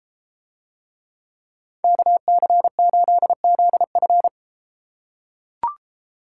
Morse code, heard as a keyed tone, about 31 wpm, 700 Hz. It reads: KC8ZF